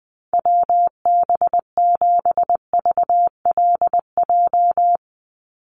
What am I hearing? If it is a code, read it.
Morse code, heard as a keyed tone, about 19 wpm, 700 Hz. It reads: WB7VLJ
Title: USA Call Signs 20wpm